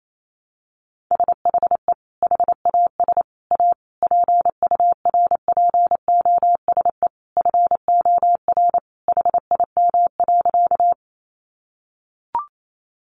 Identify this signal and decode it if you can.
Morse code, heard as a keyed tone, about 28 wpm, 700 Hz. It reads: SHE HAS A PURPOSE FOR HIM.
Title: Sentences from Top 800 Words 28wpm